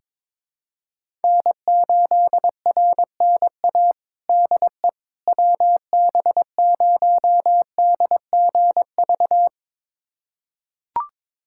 Morse code, heard as a keyed tone, about 22 wpm, 700 Hz. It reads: N8RNA DE WB0DGV
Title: USA Callsign DE Callsign 22wpm